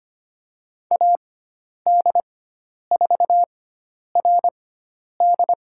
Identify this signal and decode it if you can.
Morse code, heard as a keyed tone, about 25 wpm, spaced as for 10 wpm, 700 Hz. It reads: AD4RD
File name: Lesson 16g - Letter c - Call Signs 25wpm